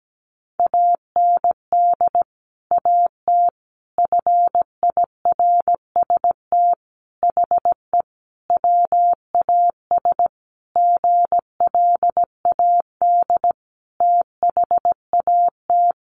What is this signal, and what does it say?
Morse code, heard as a keyed tone, about 17 wpm, 700 Hz. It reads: AND AT FIRST HE WAS GLAD THAT
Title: Sentences from Top 1300 Words 17wpm